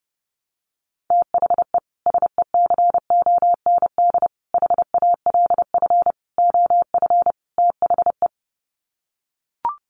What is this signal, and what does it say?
Morse code, heard as a keyed tone, about 30 wpm, 700 Hz. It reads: THE SECOND HALF OF THE
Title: Sets of 5 Words 30wpm